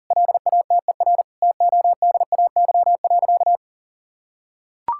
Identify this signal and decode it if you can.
Morse code, heard as a keyed tone, about 40 wpm, 700 Hz. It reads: LATER TODAY.
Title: Sentences from Top 1000 Words - Review Part 6 - 40wpm